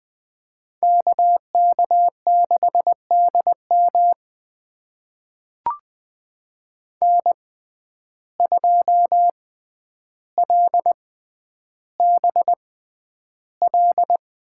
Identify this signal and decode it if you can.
Morse code, heard as a keyed tone, about 20 wpm, spaced as for 10 wpm, 700 Hz. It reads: KK6DM N2LBL
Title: USA Call Signs - 5 Characters - 6x Character Spacing - 20wpm